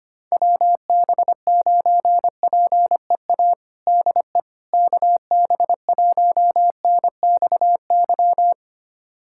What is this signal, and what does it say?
Morse code, heard as a keyed tone, about 25 wpm, 700 Hz. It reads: WB9PEA DE KB1NXY
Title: USA Callsign DE Callsign - 6 Characters - 25wpm